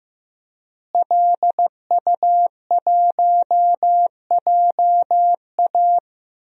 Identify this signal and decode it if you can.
Morse code, heard as a keyed tone, about 15 wpm, 700 Hz. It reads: LU1JA
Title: DXCC-100 Call Signs - 5 Characters - Repeated 3x - 15wpm